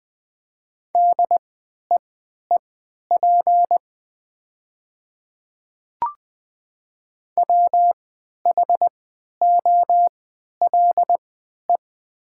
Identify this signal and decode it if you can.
Morse code, heard as a keyed tone, about 20 wpm, spaced as for 11 wpm, 700 Hz. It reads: DEEP WHOLE
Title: 3x Character Spacing - Top 500 Words - 20wpm